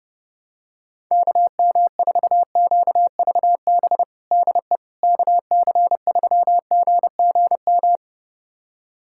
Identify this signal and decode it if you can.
Morse code, heard as a keyed tone, about 32 wpm, 700 Hz. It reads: KM4QVB DE KC3GGM
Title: USA Callsign DE Callsign - 6 Characters - 30wpm